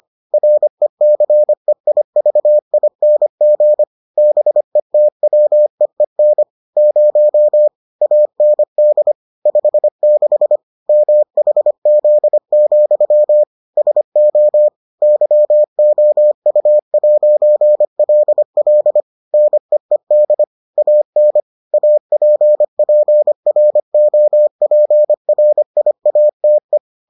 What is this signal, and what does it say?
Morse code, heard as a keyed tone, about 25 wpm, 600 Hz. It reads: RECEIVING BETWEEN 0 AND 56 MHZ, SO YOU'LL NEED AN APPROPRIATE